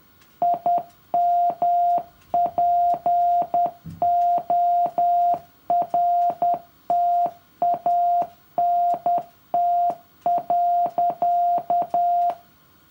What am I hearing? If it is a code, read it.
Morse code, heard as a keyed tone, about 10 wpm, 700 Hz. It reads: IMPORTANT.
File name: morse-code-for-year-3